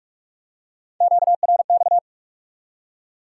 Morse code, heard as a keyed tone, about 42 wpm, 700 Hz. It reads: QRX
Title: Top 180 QSO Elements 45wpm